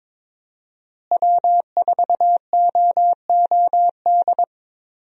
Morse code, heard as a keyed tone, about 21 wpm, 700 Hz. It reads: W4OOD
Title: USA Call Signs - 5 Characters - Repeated 2x - 22wpm